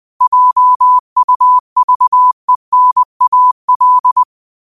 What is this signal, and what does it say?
Morse code, heard as a keyed tone, about 20 wpm, 1000 Hz. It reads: JUVENAL
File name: qotdfiles / 20wpm